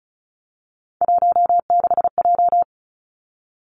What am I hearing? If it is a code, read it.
Morse code, heard as a keyed tone, about 34 wpm, 700 Hz. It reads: 16J
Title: Three Challenging Letter-Number Combinations 35wpm